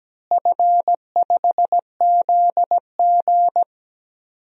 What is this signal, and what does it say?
Morse code, heard as a keyed tone, about 17 wpm, 700 Hz. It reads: F5ZG
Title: DXCC-100 Call Signs - 4 Characters - Repeated 2x - 17wpm